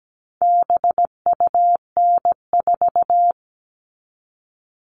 Morse code, heard as a keyed tone, about 17 wpm, 700 Hz. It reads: BUN4
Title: All Cut Numbers - 4-Digit 17wpm